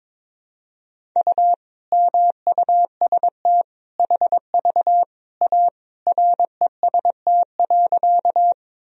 Morse code, heard as a keyed tone, about 22 wpm, 700 Hz. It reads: U MUST HV A REST.
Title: Sentences from Top 500 Words and Abbreviations 22wpm